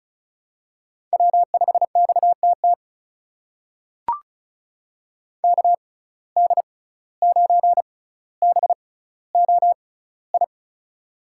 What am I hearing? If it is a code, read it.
Morse code, heard as a keyed tone, about 36 wpm, spaced as for 17 wpm, 700 Hz. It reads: W5XTT KD9BOI
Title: USA Call Signs - 6x Character Spacing - 35wpm